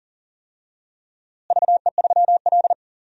Morse code, heard as a keyed tone, about 40 wpm, 700 Hz. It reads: VE3L
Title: DXCC-100 Call Signs - 4 Characters - Repeated 3x - 40wpm